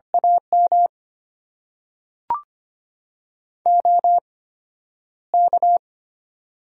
Morse code, heard as a keyed tone, about 25 wpm, spaced as for 11 wpm, 700 Hz. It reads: AM OK
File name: Top 90 QSO Elements - 8x Character Spacing - 25wpm